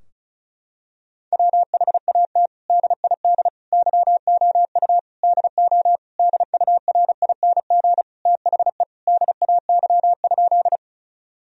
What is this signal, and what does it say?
Morse code, heard as a keyed tone, about 36 wpm, 700 Hz. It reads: WHAT DID YOU DO DURING THE DAY?
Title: Sentences from Top 1000 Words - Review Part 2 - 35wpm